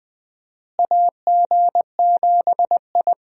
Morse code, heard as a keyed tone, about 20 wpm, 700 Hz. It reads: AG7I
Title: USA Call Signs - 4 Characters - Repeated 3x - 20wpm